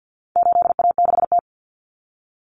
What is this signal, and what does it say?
Morse code, heard as a keyed tone, about 50 wpm, 700 Hz. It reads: 8A6T
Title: Common Cut Numbers - 4-Digit 50wpm